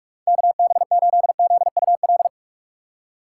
Morse code, heard as a keyed tone, about 45 wpm, 700 Hz. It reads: KB8ZUL